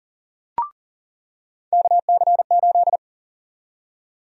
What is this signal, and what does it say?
Morse code, heard as a keyed tone, about 40 wpm, 700 Hz. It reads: KC8
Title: USA Call Sign Prefixes - 40wpm